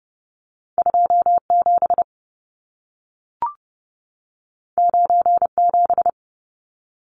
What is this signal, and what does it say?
Morse code, heard as a keyed tone, about 30 wpm, 700 Hz. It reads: 27 97
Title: Numbers - 2-Digit 30wpm